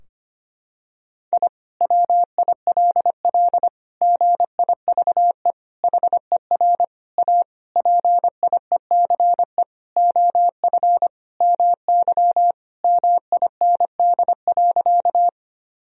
Morse code, heard as a keyed tone, about 25 wpm, 700 Hz. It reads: I WILL GIVE HER A PIECE OF MY MIND.
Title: Sentences from Top 1000 Words - Review Part 2 - 25wpm